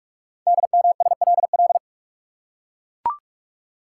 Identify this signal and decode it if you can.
Morse code, heard as a keyed tone, about 44 wpm, 700 Hz. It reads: DMSLL
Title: Code Groups - 5 Letter:Numbers - 45wpm